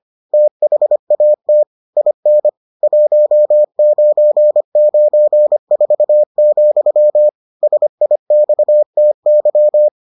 Morse code, heard as a keyed tone, about 25 wpm, 600 Hz. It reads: THAT IN 1994, SIXTY